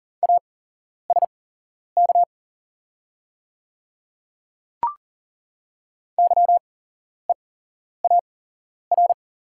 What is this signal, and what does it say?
Morse code, heard as a keyed tone, about 40 wpm, spaced as for 11 wpm, 700 Hz. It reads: ASK YEAR
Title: 8x Character Spacing - Top 300 Words - 40wpm